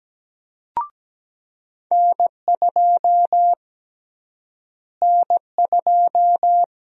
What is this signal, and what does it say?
Morse code, heard as a keyed tone, about 17 wpm, 700 Hz. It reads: N2 N2
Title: USA Call Sign Prefixes - Repeated 2x - 17wpm